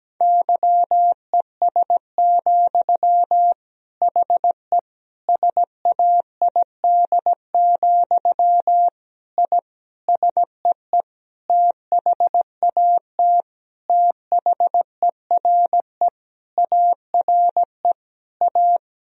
Morse code, heard as a keyed tone, about 17 wpm, 700 Hz. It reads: YES, HE SAID, I SEE THAT THERE ARE A
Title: Sentences from Top 1000 Words - Review Part 1 - 17wpm